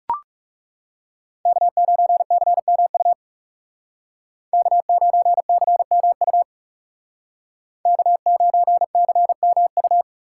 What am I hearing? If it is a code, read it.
Morse code, heard as a keyed tone, about 38 wpm, 700 Hz. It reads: K9CMU K9CMU K9CMU